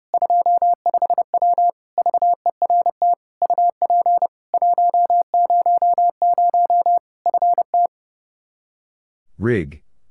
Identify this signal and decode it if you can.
Morse code, heard as a keyed tone, about 30 wpm, 700 Hz. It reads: 25W VERT UP 100 FT
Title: Practice QSO - Exchange 2 - Info - 30wpm